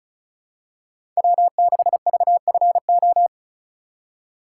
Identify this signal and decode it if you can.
Morse code, heard as a keyed tone, about 36 wpm, 700 Hz. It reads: W6VFO
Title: USA Call Signs - 5 Characters - 4x Character Spacing - 35wpm